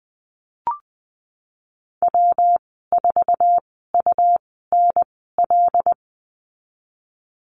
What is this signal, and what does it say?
Morse code, heard as a keyed tone, about 20 wpm, spaced as for 15 wpm, 700 Hz. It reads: W4UNL
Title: 238 Lesson 38g - Letter q - Call Signs 20wpm